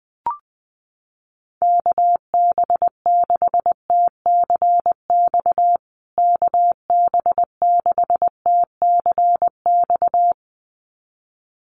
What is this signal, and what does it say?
Morse code, heard as a keyed tone, about 20 wpm, 700 Hz. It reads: KB6TCX KB6TCX
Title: USA Call Signs Repeated 2x 20wpm